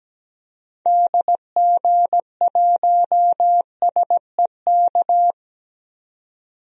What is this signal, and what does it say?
Morse code, heard as a keyed tone, about 17 wpm, 700 Hz. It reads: DG1SEK
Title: DXCC-100 Call Signs - 6 Characters - 17wpm